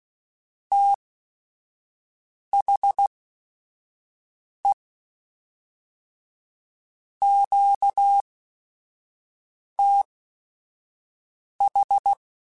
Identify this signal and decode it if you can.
Morse code, heard as a keyed tone, about 16 wpm, spaced as for 5 wpm, 800 Hz. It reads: THE QTH